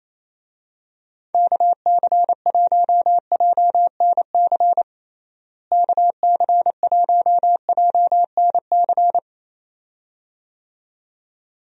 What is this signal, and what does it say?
Morse code, heard as a keyed tone, about 28 wpm, 700 Hz. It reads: KC1JNC KC1JNC
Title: USA Call Signs - 6 Characters - Repeated 2x - 28wpm